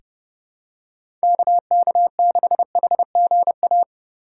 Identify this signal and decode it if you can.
Morse code, heard as a keyed tone, about 30 wpm, 700 Hz. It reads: KK6HGA
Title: USA Call Signs - 6 Characters - 2x Character Spacing - 30wpm